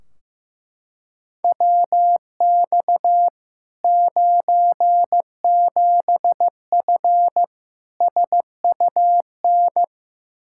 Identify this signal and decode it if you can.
Morse code, heard as a keyed tone, about 15 wpm, 700 Hz. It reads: WX 97F SUN
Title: Practice QSO - Exchange 2 - Info - 15wpm